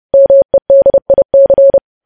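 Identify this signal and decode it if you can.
Morse code, heard as a keyed tone, about 30 wpm, 550 Hz. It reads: MEDIC
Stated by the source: Morse code practice of news headlines sent at 30 wpm.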